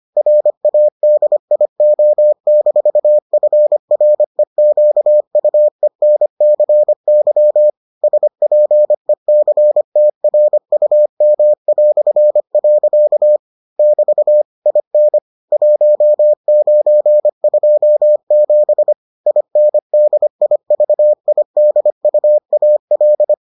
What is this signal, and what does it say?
Morse code, heard as a keyed tone, about 25 wpm, 600 Hz. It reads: RADIO-FREQUENCY SPECTRUM". <BT> IN 1927 INDIVIDUAL